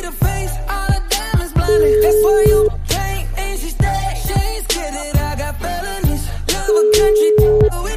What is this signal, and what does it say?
Ringback tone (Europe, 425 Hz).